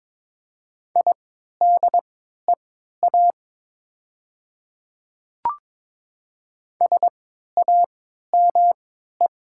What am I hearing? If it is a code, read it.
Morse code, heard as a keyed tone, about 22 wpm, spaced as for 12 wpm, 700 Hz. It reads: IDEA SAME
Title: 3x Character Spacing - Top 300 Words - 22wpm